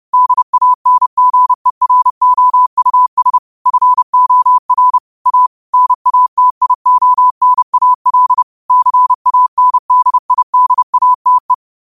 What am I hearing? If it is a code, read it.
Morse code, heard as a keyed tone, about 30 wpm, 1000 Hz. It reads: DANGEROUS FOR A NATIONAL CANDIDATE
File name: qotdfiles / 30wpm